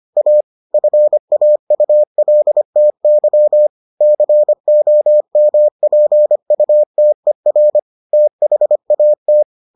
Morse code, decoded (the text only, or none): A FAULTY COMPUTER THAT